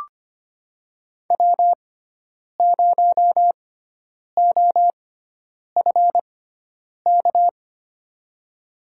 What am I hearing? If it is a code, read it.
Morse code, heard as a keyed tone, about 25 wpm, spaced as for 9 wpm, 700 Hz. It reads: W0OFK
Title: USA Call Signs - 6x Character Spacing - 25wpm